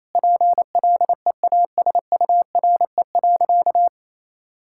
Morse code, heard as a keyed tone, about 28 wpm, 700 Hz. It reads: PLEASURE.
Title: Sentences from Top 1600 Words 28wpm